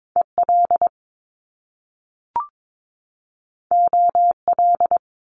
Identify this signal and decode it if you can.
Morse code, heard as a keyed tone, about 22 wpm, 700 Hz. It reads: EL OL
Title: Top Two Letter Combinations in Words 22wpm